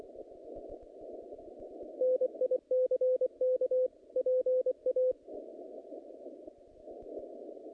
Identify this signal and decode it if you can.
Morse code, heard as a keyed tone, about 25 wpm, 500 Hz. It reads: NICK PA
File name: contests / naqp-cw-k5tr-op-k5pi- -0000